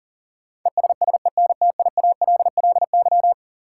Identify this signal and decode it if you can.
Morse code, heard as a keyed tone, about 40 wpm, 700 Hz. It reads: ESSENTIALLY
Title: Sentences from Top 1300 Words - New Word Drill - 40wpm